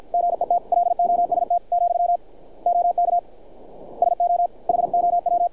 Morse code, heard as a keyed tone, about 50 wpm, 700 Hz. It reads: GET LOST , OK SO HOW
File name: archived / CRSnet